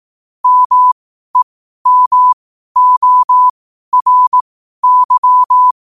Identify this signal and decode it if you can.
Morse code, heard as a keyed tone, about 18 wpm, spaced as for 12 wpm, 1000 Hz. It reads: MEMORY
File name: qotdfiles / 12wpm